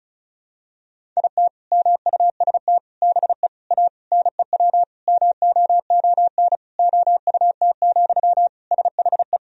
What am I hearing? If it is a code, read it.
Morse code, heard as a keyed tone, about 36 wpm, 700 Hz. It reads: IT MUST BE A NEW MOON OUT, SHE
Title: Sentences from Top 500 Words and Abbreviations 35wpm